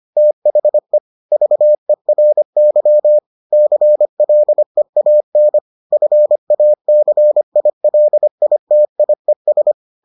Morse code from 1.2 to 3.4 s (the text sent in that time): VERY